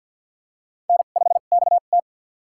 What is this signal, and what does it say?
Morse code, heard as a keyed tone, about 49 wpm, spaced as for 35 wpm, 700 Hz. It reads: N5XT
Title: USA Call Signs - 2x Character Spacing - 50wpm